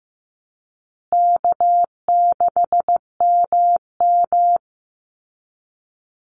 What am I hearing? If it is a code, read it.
Morse code, heard as a keyed tone, about 15 wpm, 700 Hz. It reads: K6MM
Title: Call Signs Repeated 2x - 15wpm